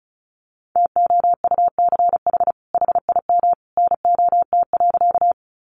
Morse code, heard as a keyed tone, about 34 wpm, 700 Hz. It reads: TOUCH HIM NOT.
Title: Sentences from Top 1200 Words 35wpm